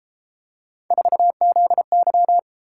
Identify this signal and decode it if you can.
Morse code, heard as a keyed tone, about 31 wpm, 700 Hz. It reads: 4ZY